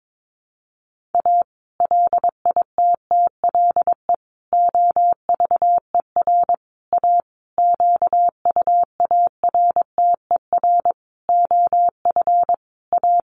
Morse code, heard as a keyed tone, about 22 wpm, 700 Hz. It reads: A LITTLE OVER A QUARTER OF A
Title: Sentences from Top 1300 Words 22wpm